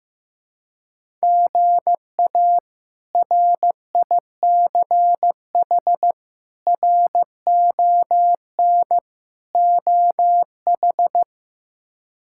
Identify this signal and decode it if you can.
Morse code, heard as a keyed tone, about 15 wpm, 700 Hz. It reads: GA RICH RON OH
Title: K1USN SST QSO Practice - 15wpm